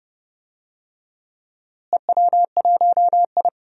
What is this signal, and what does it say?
Morse code, heard as a keyed tone, about 30 wpm, 700 Hz. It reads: EW1I
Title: DXCC-100 Call Signs - 4 Characters - Repeated 2x - 30wpm